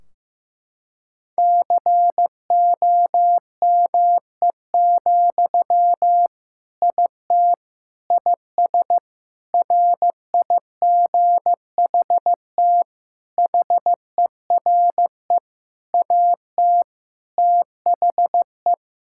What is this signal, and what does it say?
Morse code, heard as a keyed tone, about 14 wpm, 700 Hz. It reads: COME, IT IS RIGHT HERE AT THE
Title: Sentences from Top 1600 Words 15wpm